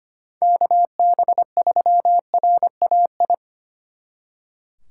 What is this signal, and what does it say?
Morse code, heard as a keyed tone, about 33 wpm, 700 Hz. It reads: KB3RAI